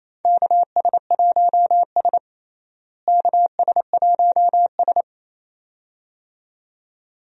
Morse code, heard as a keyed tone, about 28 wpm, 700 Hz. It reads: KS1S KS1S
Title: USA Call Signs - 4 Characters - Repeated 2x - 28wpm